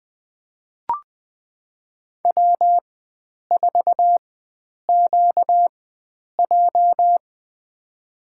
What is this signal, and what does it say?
Morse code, heard as a keyed tone, about 20 wpm, spaced as for 9 wpm, 700 Hz. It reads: W4QJ
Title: USA Call Signs - 4x Character Spacing - 20wpm